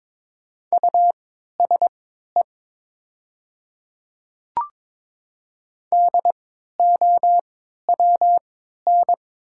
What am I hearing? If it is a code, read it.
Morse code, heard as a keyed tone, about 22 wpm, spaced as for 13 wpm, 700 Hz. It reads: USE DOWN
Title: 3x Character Spacing - Top 200 Words - 22wpm